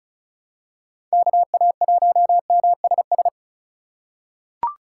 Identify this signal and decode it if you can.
Morse code, heard as a keyed tone, about 36 wpm, 700 Hz. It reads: KA1MSS
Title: USA Call Signs - 4x Character Spacing - 35wpm